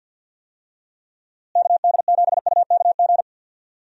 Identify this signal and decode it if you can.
Morse code, heard as a keyed tone, about 52 wpm, 700 Hz. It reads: KD7UKG
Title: USA Call Signs - 6 Characters - Repeated 2x - 50wpm